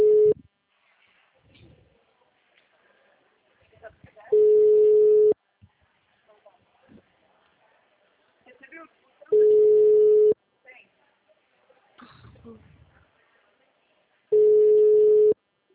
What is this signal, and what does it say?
Ringback tone (Europe, 425 Hz), with speech.
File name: out- -2012- - - .36